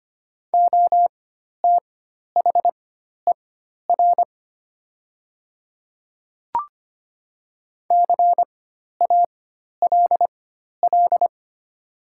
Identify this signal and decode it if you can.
Morse code, heard as a keyed tone, about 25 wpm, spaced as for 12 wpm, 700 Hz. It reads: OTHER CALL
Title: 4x Character Spacing - Top 200 Words - 25wpm